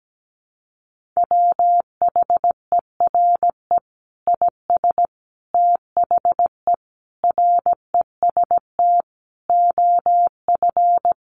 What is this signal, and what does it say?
Morse code, heard as a keyed tone, about 17 wpm, 700 Hz. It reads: WHERE IS THE REST OF
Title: Sentences from Top 1500 Words 17wpm